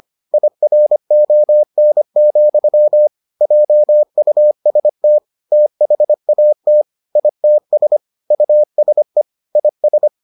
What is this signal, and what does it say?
Morse code, heard as a keyed tone, about 25 wpm, 600 Hz. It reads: IRON, JUST THAT ITS USE IS